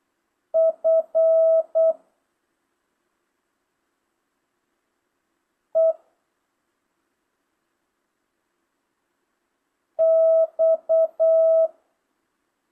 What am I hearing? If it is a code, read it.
Morse code, heard as a keyed tone, about 8 wpm, 650 Hz. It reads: F E X
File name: jh8cbh / 2025cw